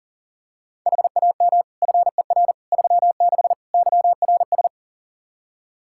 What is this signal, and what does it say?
Morse code, heard as a keyed tone, about 37 wpm, 700 Hz. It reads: HAM FER 36 YRS
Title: Practice QSO - Exchange 3 - More Info - 40wpm